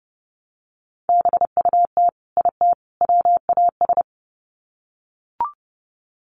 Morse code, heard as a keyed tone, about 30 wpm, 700 Hz. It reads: BUT IT WAS
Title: Sets of 3 Words 30wpm